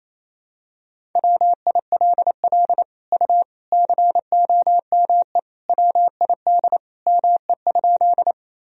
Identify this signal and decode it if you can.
Morse code, heard as a keyed tone, about 27 wpm, 700 Hz. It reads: WILL U COME WID ME?
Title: Sentences from Top 200 Words and Abbreviations 28wpm